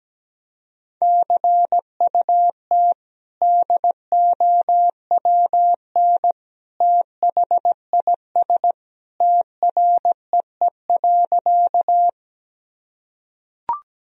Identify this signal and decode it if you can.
Morse code, heard as a keyed tone, about 17 wpm, 700 Hz. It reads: CUT DOWN THIS TREE.
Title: Sentences from Top 1000 Words - Review Part 1 - 17wpm